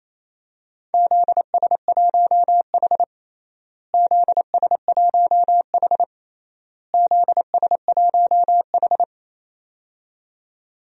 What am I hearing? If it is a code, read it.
Morse code, heard as a keyed tone, about 28 wpm, 700 Hz. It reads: ZS1H ZS1H ZS1H
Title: DXCC-100 Call Signs - 4 Characters - Repeated 3x - 28wpm